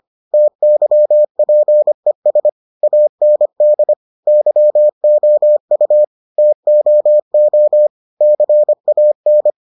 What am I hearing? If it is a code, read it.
Morse code, heard as a keyed tone, about 25 wpm, 600 Hz. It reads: TYPES AND YOU TOO CAN